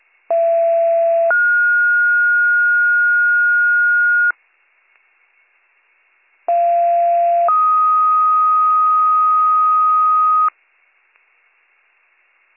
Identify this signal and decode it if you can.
Two-tone page, heard as a one-second tone, then a three-second tone, about 670 then 1430 Hz, about 690 then 1250 Hz. It is heard from 0.3 to 10.5 s.